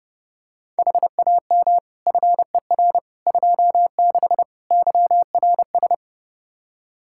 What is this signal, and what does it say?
Morse code, heard as a keyed tone, about 29 wpm, 700 Hz. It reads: HAM FER 26 YRS
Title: Practice QSO - Exchange 3 - More Info - 30wpm